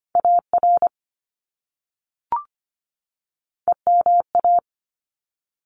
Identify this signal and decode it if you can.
Morse code, heard as a keyed tone, about 25 wpm, 700 Hz. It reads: AR EMA
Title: ARRL Sections 25wpm